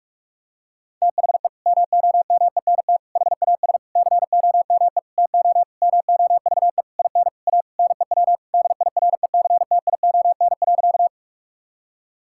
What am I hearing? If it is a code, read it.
Morse code, heard as a keyed tone, about 41 wpm, 700 Hz. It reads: THE MOMENT HAS COME TO MOVE IN A NEW DIRECTION.
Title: Sentences from Top 1000 Words - Review Part 5 - 45wpm